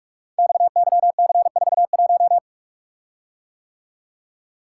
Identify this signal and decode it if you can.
Morse code, heard as a keyed tone, about 48 wpm, 690 Hz. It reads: XYC41
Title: Code Groups - 5 Letter:Numbers - 45wpm